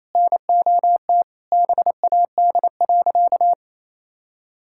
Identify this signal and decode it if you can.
Morse code, heard as a keyed tone, about 27 wpm, 700 Hz. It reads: NOT BAD.
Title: Sentences from Top 1000 Words - Review Part 6 - 28wpm